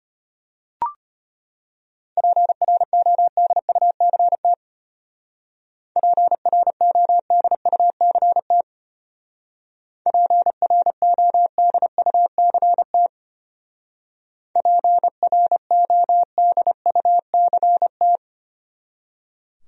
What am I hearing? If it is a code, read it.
Morse code, heard as a keyed tone, about 31 wpm, 700 Hz. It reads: PRODUCT PRODUCT PRODUCT PRODUCT
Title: Speed-Racing Top 500 Words 25wpm